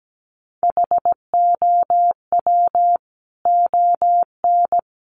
Morse code, heard as a keyed tone, about 17 wpm, 700 Hz. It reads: HOW ON
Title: Sentences from Top 200 Words 17wpm